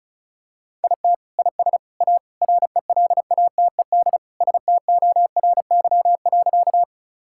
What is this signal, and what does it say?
Morse code, heard as a keyed tone, about 34 wpm, 700 Hz. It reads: IT IS A RELATED STORY.
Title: Sentences from Top 1000 Words - Review Part 5 - 35wpm